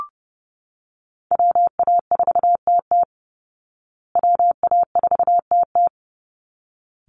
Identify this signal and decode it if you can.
Morse code, heard as a keyed tone, about 30 wpm, 700 Hz. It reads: WA4TT WA4TT
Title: Call Signs Repeated 3x - 30wpm